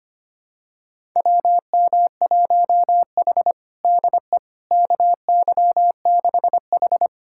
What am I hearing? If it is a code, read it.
Morse code, heard as a keyed tone, about 25 wpm, 700 Hz. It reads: WM1H DE KY6H
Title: USA Callsign DE Callsign - 4 Characters - 25wpm